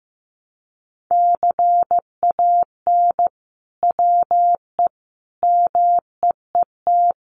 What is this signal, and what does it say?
Morse code, heard as a keyed tone, about 15 wpm, 700 Hz. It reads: CAN WE MEET
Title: Sentences from Top 1100 Words 15wpm